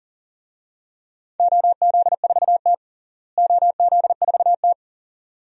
Morse code, heard as a keyed tone, about 40 wpm, 700 Hz. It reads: OZ4T OZ4T
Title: DXCC-100 Call Signs - 4 Characters - Repeated 2x - 40wpm